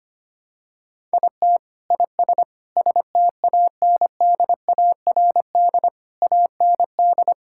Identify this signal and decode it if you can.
Morse code, heard as a keyed tone, about 25 wpm, 700 Hz. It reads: IT IS STANDARD AND
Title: Sentences from Top 1000 Words - Review Part 6 - 25wpm